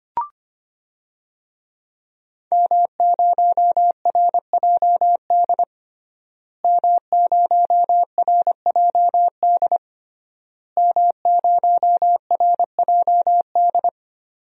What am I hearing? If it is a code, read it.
Morse code, heard as a keyed tone, about 25 wpm, 700 Hz. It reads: M0RJD M0RJD M0RJD
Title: DXCC-100 Call Signs - 5 Characters - Repeated 3x - 25wpm